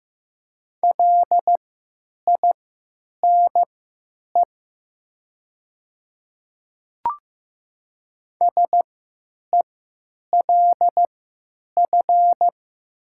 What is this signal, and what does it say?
Morse code, heard as a keyed tone, about 15 wpm, spaced as for 9 wpm, 700 Hz. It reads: LINE SELF
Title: 3x Character Spacing - Top 200 Words - 15wpm